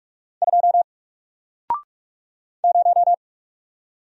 Morse code, heard as a keyed tone, about 44 wpm, 700 Hz. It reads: 2 0
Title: Single Letter-Number Alt 45wpm